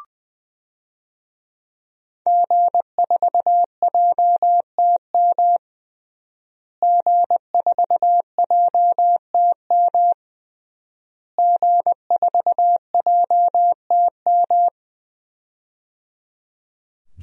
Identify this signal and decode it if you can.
Morse code, heard as a keyed tone, about 20 wpm, 700 Hz. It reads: G4JTM G4JTM G4JTM
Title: DXCC-100 Call Signs - 5 Characters - Repeated 3x - 20wpm